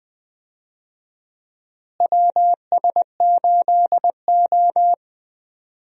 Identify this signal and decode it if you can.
Morse code, heard as a keyed tone, about 21 wpm, 700 Hz. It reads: WS8O
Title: USA Call Signs - 4 Characters - Repeated 2x - 20wpm